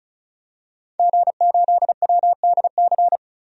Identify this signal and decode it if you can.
Morse code, heard as a keyed tone, about 34 wpm, 700 Hz. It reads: G8WDC
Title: DXCC-100 Call Signs - 5 Characters - 35wpm